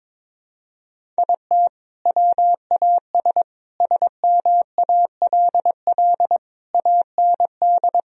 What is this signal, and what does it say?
Morse code, heard as a keyed tone, about 22 wpm, 700 Hz. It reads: IT WAS SMALL AND
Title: Sentences from Top 1000 Words - Review Part 1 - 22wpm